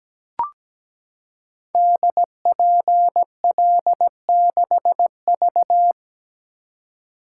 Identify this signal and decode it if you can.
Morse code, heard as a keyed tone, about 17 wpm, 700 Hz. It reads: DPL6V
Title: Code Groups - 5 Letter:Numbers - 17wpm